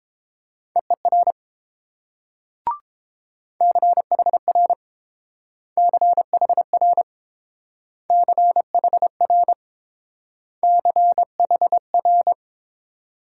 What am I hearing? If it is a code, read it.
Morse code, heard as a keyed tone, about 28 wpm, 700 Hz. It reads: EER CHR CHR CHR CHR